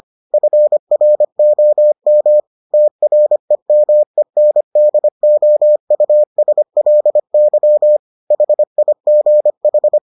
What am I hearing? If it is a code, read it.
Morse code, heard as a keyed tone, about 25 wpm, 600 Hz. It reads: FROM TREMENDOUSLY HIGH